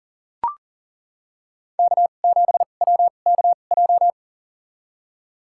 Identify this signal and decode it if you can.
Morse code, heard as a keyed tone, about 39 wpm, spaced as for 29 wpm, 700 Hz. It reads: K7WKJ
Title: USA Call Signs - 5 Characters - 2x Character Spacing - 40wpm